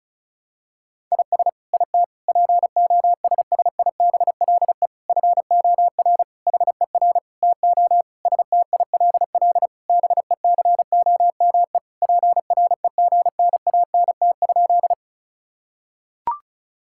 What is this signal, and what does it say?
Morse code, heard as a keyed tone, about 36 wpm, 700 Hz. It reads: IS IT POSSIBLE FOR HER TO STILL BECOME PREGNANT?
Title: Sentences from Top 1000 Words - Review Part 6 - 35wpm